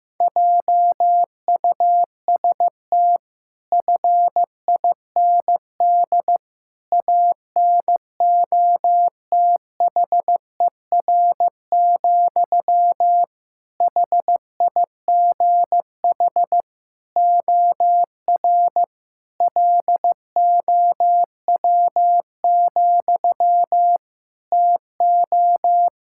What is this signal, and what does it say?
Morse code, heard as a keyed tone, about 15 wpm, 700 Hz. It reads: JUST FIND ANOTHER, HIGH OR LOW, TO
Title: Sentences from Top 1600 Words 15wpm